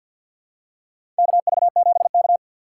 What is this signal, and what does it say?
Morse code, heard as a keyed tone, about 53 wpm, 700 Hz. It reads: KV7K
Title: USA Call Signs - 4 Characters - 6x Character Spacing - 50wpm